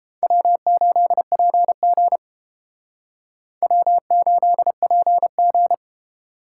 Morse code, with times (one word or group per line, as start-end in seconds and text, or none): 0.2-2.2 W8PG
3.6-5.7 W8PG